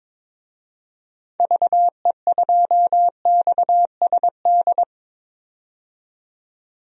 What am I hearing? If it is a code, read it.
Morse code, heard as a keyed tone, about 21 wpm, 700 Hz. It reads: VE2XSD
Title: DXCC-100 Call Signs - 6 Characters - 22wpm